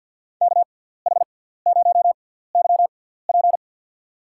Morse code, heard as a keyed tone, about 51 wpm, spaced as for 17 wpm, 700 Hz. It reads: KH0YP
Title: USA Call Signs - 5 Characters - 6x Character Spacing - 50wpm